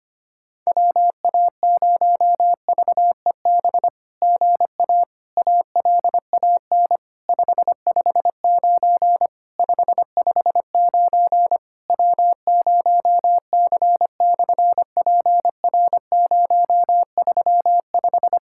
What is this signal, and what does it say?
Morse code, heard as a keyed tone, about 24 wpm, 700 Hz. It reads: WA0VEB GA ALAN 559 559 W0C/PR035
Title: SOTA QSO Practice 25wpm